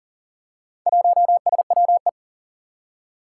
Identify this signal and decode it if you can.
Morse code, heard as a keyed tone, about 41 wpm, 700 Hz. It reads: 1SWE
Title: Four Challenging Letter-Number Combinations 40wpm